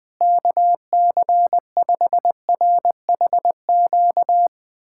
Morse code, heard as a keyed tone, about 20 wpm, 700 Hz. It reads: KC5RHQ
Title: USA Callsign DE Callsign 20wpm